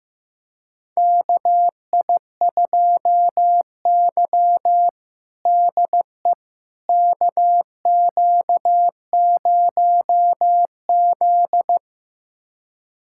Morse code, heard as a keyed tone, about 15 wpm, 700 Hz. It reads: KI2Y DE KQ0Z
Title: USA Callsign DE Callsign - 4 Characters - 15wpm